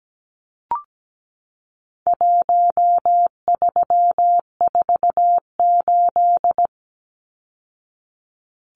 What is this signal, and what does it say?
Morse code, heard as a keyed tone, about 17 wpm, 700 Hz. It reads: 1348
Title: Numbers - Contest Style - 4-Digit 17wpm